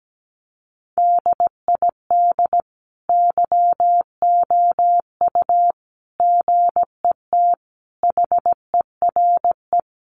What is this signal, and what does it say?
Morse code, heard as a keyed tone, about 17 wpm, 700 Hz. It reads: DID YOU GET HERE
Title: Sentences from Top 700 Words 17wpm